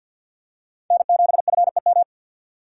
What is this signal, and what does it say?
Morse code, heard as a keyed tone, about 47 wpm, 700 Hz. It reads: N7FEM
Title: USA Call Signs - 4x Character Spacing - 50wpm